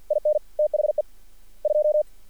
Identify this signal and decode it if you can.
Morse code, heard as a keyed tone, about 48 wpm, 600 Hz. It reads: IN THE 2
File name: archived / CRSnet